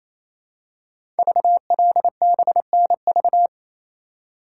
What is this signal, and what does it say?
Morse code, heard as a keyed tone, about 28 wpm, 700 Hz. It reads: VLBNV
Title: Code Groups - 5 Letter:Numbers - 28wpm